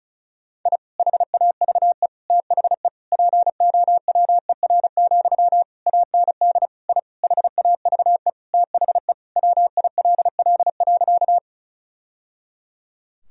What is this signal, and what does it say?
Morse code, heard as a keyed tone, about 36 wpm, 700 Hz. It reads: I HAVE THE POWER, AND I HAVE THE WILL.
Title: Sentences from Top 1000 Words - Review Part 2 - 35wpm